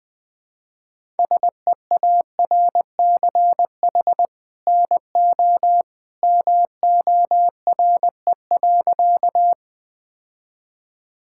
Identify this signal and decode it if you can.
Morse code, heard as a keyed tone, about 20 wpm, 700 Hz. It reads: SEARCH NO MORE.
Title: Sentences from Top 1000 Words - Review Part 5 - 20wpm